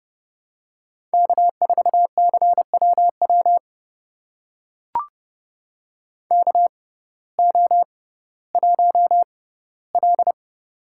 Morse code, heard as a keyed tone, about 30 wpm, spaced as for 16 wpm, 700 Hz. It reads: K4CWW KO1L
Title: USA Call Signs - 6x Character Spacing - 30wpm